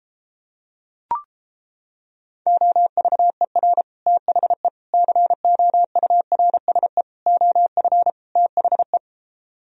Morse code, heard as a keyed tone, about 29 wpm, 700 Hz. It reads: OVER THE COURSE OF THE